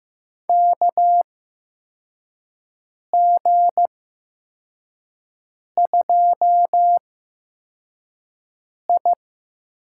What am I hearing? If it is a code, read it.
Morse code, heard as a keyed tone, about 15 wpm, spaced as for 4 wpm, 700 Hz. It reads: KG2I
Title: USA Call Signs - 8x Character Spacing - 15wpm